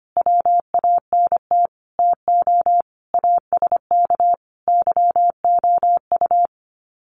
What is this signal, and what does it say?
Morse code, heard as a keyed tone, about 25 wpm, 700 Hz. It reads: WANT TO ASK YOU
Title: Sets of 5 Words 25wpm